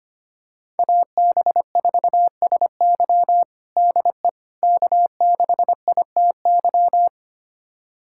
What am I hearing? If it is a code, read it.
Morse code, heard as a keyed tone, about 25 wpm, 700 Hz. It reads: AB4SY DE K6ITY
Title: USA Callsign DE Callsign - 5 Characters - 25wpm